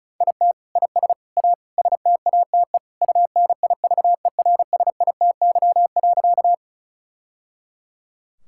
Morse code, heard as a keyed tone, about 36 wpm, 700 Hz. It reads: IT IS A STATE UNIVERSITY.
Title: Sentences from Top 1000 Words - Review Part 5 - 35wpm